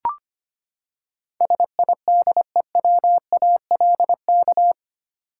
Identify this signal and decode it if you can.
Morse code, heard as a keyed tone, about 25 wpm, 700 Hz. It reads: SIDEWALK
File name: Eight Letter Words - 25wpm